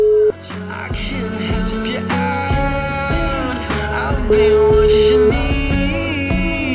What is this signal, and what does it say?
Ringback tone (Europe, 425 Hz), with noise.